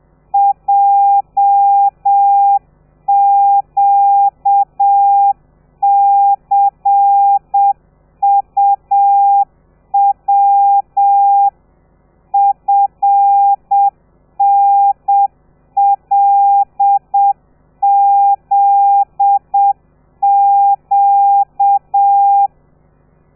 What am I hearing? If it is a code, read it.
Morse code, heard as a keyed tone, about 7 wpm, 800 Hz. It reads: JQCUW FNLZQ
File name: telegrafia / i2tfj / Lez13 30